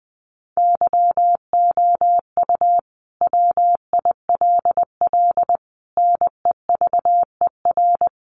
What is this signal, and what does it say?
Morse code, heard as a keyed tone, about 19 wpm, 700 Hz. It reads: YOU WILL NEVER
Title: Sentences from Top 1200 Words 20wpm